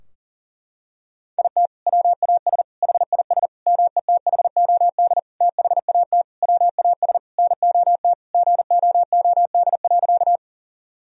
Morse code, heard as a keyed tone, about 40 wpm, 700 Hz. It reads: IT WAS HIS METHOD THAT WAS NOT GOOD.
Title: Sentences from Top 1000 Words - Review Part 3 - 40wpm